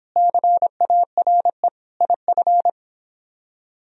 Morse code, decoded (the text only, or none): CARE IF